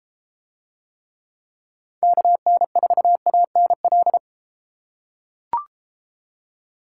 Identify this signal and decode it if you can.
Morse code, heard as a keyed tone, about 32 wpm, 700 Hz. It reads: KN4ANL